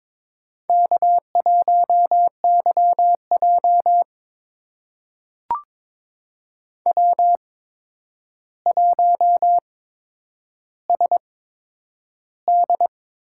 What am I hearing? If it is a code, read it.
Morse code, heard as a keyed tone, about 22 wpm, spaced as for 9 wpm, 700 Hz. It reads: K1YJ W1SD
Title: USA Call Signs - 4 Characters - 8x Character Spacing - 22wpm